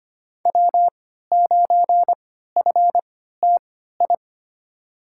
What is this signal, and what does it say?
Morse code, heard as a keyed tone, about 25 wpm, spaced as for 14 wpm, 700 Hz. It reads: W9FTI
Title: USA Call Signs - 5 Characters - 3x Character Spacing - 25wpm